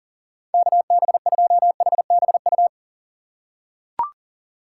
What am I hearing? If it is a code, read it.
Morse code, heard as a keyed tone, about 40 wpm, 700 Hz. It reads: KB2HBU
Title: USA Call Signs - 6 Characters - 4x Character Spacing - 40wpm